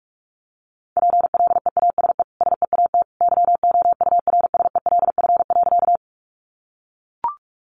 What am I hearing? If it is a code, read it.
Morse code, heard as a keyed tone, about 44 wpm, 700 Hz. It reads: PLEASE SEAT YOURSELF.
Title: Sentences from Top 1500 Words 45wpm